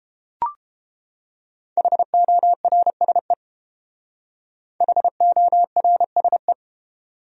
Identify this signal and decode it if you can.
Morse code, heard as a keyed tone, about 29 wpm, 700 Hz. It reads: HORSE HORSE